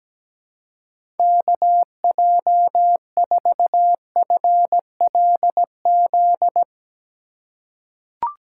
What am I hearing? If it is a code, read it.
Morse code, heard as a keyed tone, about 17 wpm, 700 Hz. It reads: KJ4FLZ
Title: USA Call Signs - 6 Characters - 2x Character Spacing - 17wpm